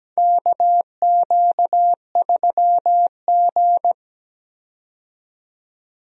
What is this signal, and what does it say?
Morse code, heard as a keyed tone, about 17 wpm, 700 Hz. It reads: KQ3G
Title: USA Call Signs - 4 Characters - 17wpm